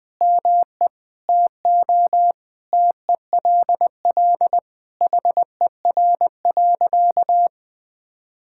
Morse code, heard as a keyed tone, about 21 wpm, 700 Hz. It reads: ME TO TELL HER.
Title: Sentences from Top 1000 Words - Review Part 5 - 20wpm